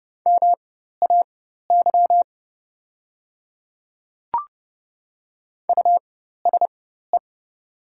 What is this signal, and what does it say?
Morse code, heard as a keyed tone, about 30 wpm, spaced as for 14 wpm, 700 Hz. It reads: MAY USE
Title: 4x Character Spacing - Top 300 Words - 30wpm